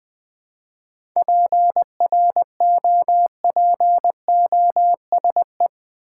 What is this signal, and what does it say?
Morse code, heard as a keyed tone, about 19 wpm, 700 Hz. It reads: PROPOSE
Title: Seven Letter Words - 20wpm